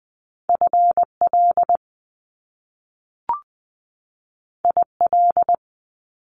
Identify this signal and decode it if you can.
Morse code, heard as a keyed tone, about 20 wpm, 700 Hz. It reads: F L I L
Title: Top Two Letter Combinations in Words 20wpm